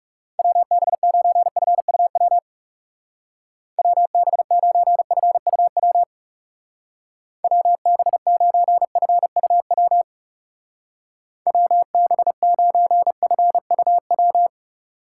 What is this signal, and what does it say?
Morse code, heard as a keyed tone, about 38 wpm, 700 Hz. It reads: WB9FUW WB9FUW WB9FUW WB9FUW